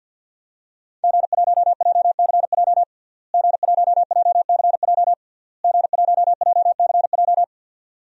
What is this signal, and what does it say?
Morse code, heard as a keyed tone, about 51 wpm, 700 Hz. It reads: G1JCJ G1JCJ G1JCJ
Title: DXCC-100 Call Signs - 5 Characters - Repeated 3x - 50wpm